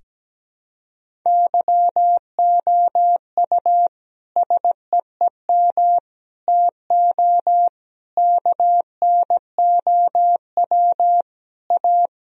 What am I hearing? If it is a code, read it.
Morse code, heard as a keyed tone, about 17 wpm, 700 Hz. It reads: YOU SEEM TO KNOW A
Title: Sentences from Top 1000 Words - Review Part 1 - 17wpm